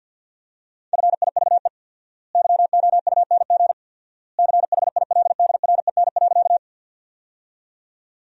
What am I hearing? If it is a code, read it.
Morse code, heard as a keyed tone, about 50 wpm, 700 Hz. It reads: FIVE YOUNG CHILDREN.
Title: 4x Word-Spacing - Sentences from Top 400 Words - 50wpm